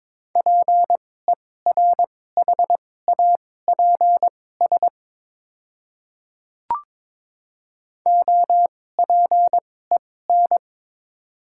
Morse code, heard as a keyed tone, about 23 wpm, spaced as for 16 wpm, 690 Hz. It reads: PERHAPS OPEN
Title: 2x Character Spacing - Top 500 Words - 22wpm